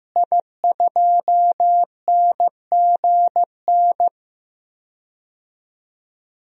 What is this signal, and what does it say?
Morse code, heard as a keyed tone, about 15 wpm, 700 Hz. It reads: I2NGN
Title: DXCC-100 Call Signs - 5 Characters - Repeated 2x - 15wpm